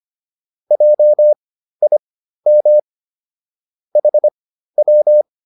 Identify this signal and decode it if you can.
Morse code, heard as a keyed tone, about 25 wpm, spaced as for 13 wpm, 600 Hz. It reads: JIM HW